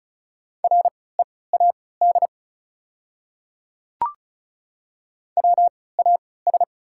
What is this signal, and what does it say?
Morse code, heard as a keyed tone, about 36 wpm, spaced as for 20 wpm, 690 Hz. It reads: READ WAS
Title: 3x Character Spacing - Top 500 Words - 35wpm